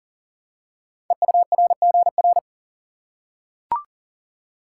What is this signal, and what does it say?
Morse code, heard as a keyed tone, about 38 wpm, 700 Hz. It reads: EURGR
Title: Code Groups - 5 Letters - 40wpm